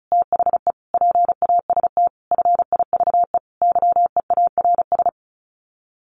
Morse code, heard as a keyed tone, about 34 wpm, 700 Hz. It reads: THE PAST FIVE YEARS
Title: Sets of 5 Words Encore 35wpm